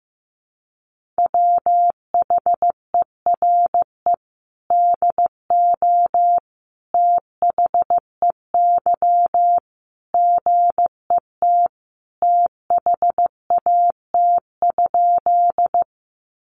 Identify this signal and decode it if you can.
Morse code, heard as a keyed tone, about 15 wpm, 700 Hz. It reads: WHERE DO THEY GET THAT?
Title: Sentences from Top 200 Words 15wpm